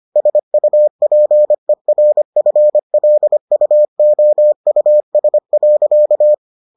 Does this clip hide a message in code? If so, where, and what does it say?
Yes: Morse code from 0.2 to 6.3 s, heard as a keyed tone, about 25 wpm, 600 Hz. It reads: SUPERFLUOUS.